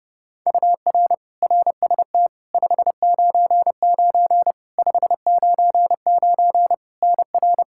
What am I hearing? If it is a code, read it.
Morse code, heard as a keyed tone, about 30 wpm, 700 Hz. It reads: UR RST 599 599 NR
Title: K1USN SST QSO Practice 30wpm